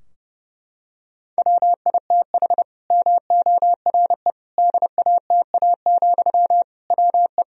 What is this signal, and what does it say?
Morse code, heard as a keyed tone, about 30 wpm, 700 Hz. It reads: WITH MORE DATA, WE
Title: Sentences from Top 1000 Words - Review Part 3 - 30wpm